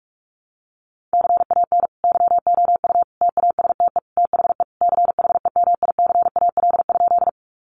Morse code, heard as a keyed tone, about 44 wpm, 700 Hz. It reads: CAN YOU TASTE THE CHEMICAL?
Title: Sentences from Top 1200 Words 45wpm